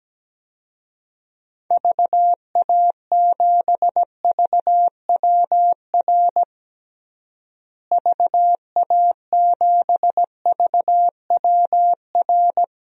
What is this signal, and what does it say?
Morse code, heard as a keyed tone, about 17 wpm, 700 Hz. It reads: VA7VWR VA7VWR
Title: DXCC-100 Call Signs - 6 Characters - Repeated 2x - 17wpm